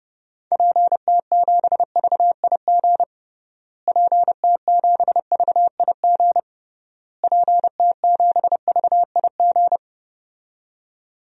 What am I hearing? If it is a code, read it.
Morse code, heard as a keyed tone, about 30 wpm, 700 Hz. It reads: PT7VIG PT7VIG PT7VIG
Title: DXCC-100 Call Signs - 6 Characters - Repeated 3x - 30wpm